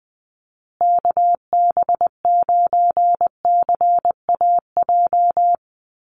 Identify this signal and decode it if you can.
Morse code, heard as a keyed tone, about 20 wpm, 700 Hz. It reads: KB9CAJ
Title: USA Call Signs Repeated 2x 20wpm